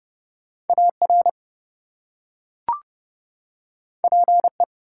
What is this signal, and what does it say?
Morse code, heard as a keyed tone, about 28 wpm, 700 Hz. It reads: AR PE
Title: US States and Canadian Provinces 30wpm